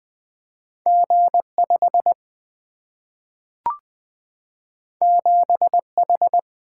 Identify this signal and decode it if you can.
Morse code, heard as a keyed tone, about 20 wpm, 700 Hz. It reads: G 5 7 H
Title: Single-Letter Single-Number 20wpm